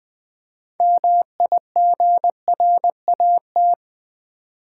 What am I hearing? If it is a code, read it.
Morse code, heard as a keyed tone, about 20 wpm, 700 Hz. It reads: MIGRAT
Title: Build A Word - 3 Letters At A Time - 9-Letter Words - 20wpm